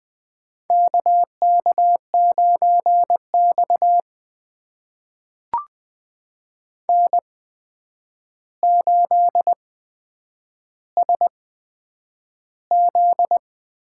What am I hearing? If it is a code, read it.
Morse code, heard as a keyed tone, about 20 wpm, spaced as for 9 wpm, 700 Hz. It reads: KK9X N8SZ
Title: USA Call Signs - 4 Characters - 8x Character Spacing - 20wpm